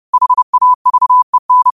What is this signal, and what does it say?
Morse code, heard as a keyed tone, about 30 wpm, 1000 Hz. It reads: HAVEN